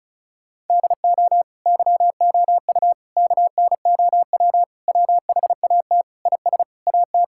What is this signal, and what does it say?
Morse code, heard as a keyed tone, about 34 wpm, 700 Hz. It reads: DO YOU KNOW WHAT IS AT
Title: Sentences from Top 1000 Words - Review Part 6 - 35wpm